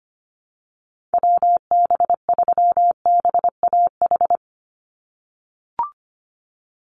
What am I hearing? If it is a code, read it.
Morse code, heard as a keyed tone, about 25 wpm, 690 Hz. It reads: WB3BAH
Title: USA Call Signs 25wpm